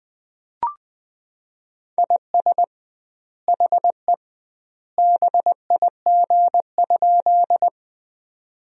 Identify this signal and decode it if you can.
Morse code, heard as a keyed tone, about 20 wpm, 700 Hz. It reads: IS HE BIG?
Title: 2x Word-Spacing - Sentences from Top 200 Words - 20wpm